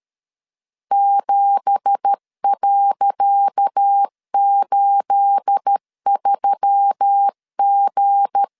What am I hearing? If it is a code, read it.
Morse code, heard as a keyed tone, about 12 wpm, 800 Hz. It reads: 7.83G
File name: LEZIONI / 10 SIGNS / 10C DOT 12wpm 2451